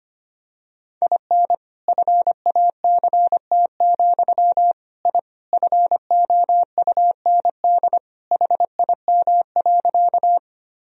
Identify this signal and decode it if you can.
Morse code, heard as a keyed tone, about 25 wpm, 700 Hz. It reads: IN FACT, I FOUND HIM.
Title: Sentences from Top 1000 Words - Review Part 3 - 25wpm